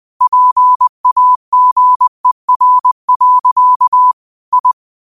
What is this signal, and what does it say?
Morse code, heard as a keyed tone, about 20 wpm, 1000 Hz. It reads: PAGER. I
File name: qotdfiles / 20wpm